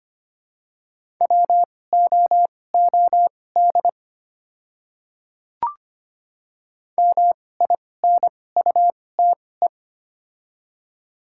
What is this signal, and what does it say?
Morse code, heard as a keyed tone, about 25 wpm, spaced as for 18 wpm, 700 Hz. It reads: WOOD MINUTE
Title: 2x Character Spacing - Top 500 Words - 25wpm